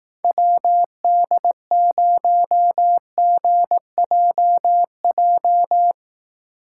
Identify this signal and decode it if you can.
Morse code, heard as a keyed tone, about 17 wpm, 700 Hz. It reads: WD0GJJ
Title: Speed Racing - USA 2x3 Call Signs 15wpm